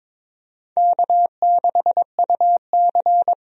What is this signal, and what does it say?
Morse code, heard as a keyed tone, about 23 wpm, 700 Hz. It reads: K6UC
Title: USA Call Signs - 4 Characters - 22wpm